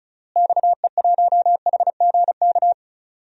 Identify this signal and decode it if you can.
Morse code, heard as a keyed tone, about 34 wpm, 700 Hz. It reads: XE1HGK
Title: DXCC-100 Call Signs - 6 Characters - Repeated 3x - 35wpm